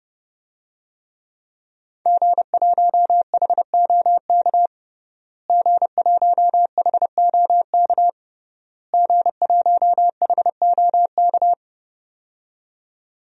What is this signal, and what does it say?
Morse code, heard as a keyed tone, about 32 wpm, 700 Hz. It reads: G1HOK G1HOK G1HOK
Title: DXCC-100 Call Signs - 5 Characters - Repeated 3x - 30wpm